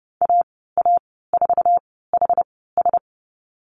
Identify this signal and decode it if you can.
Morse code, heard as a keyed tone, about 30 wpm, spaced as for 17 wpm, 700 Hz. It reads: AA4HS
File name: Lesson 19g - Letter w - Call Signs 30wpm